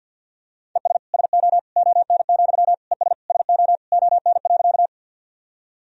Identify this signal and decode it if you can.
Morse code, heard as a keyed tone, about 50 wpm, 700 Hz. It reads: ES SO ON, ES SO ON.
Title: Sentences from Top 100 Words and Abbreviations 50wpm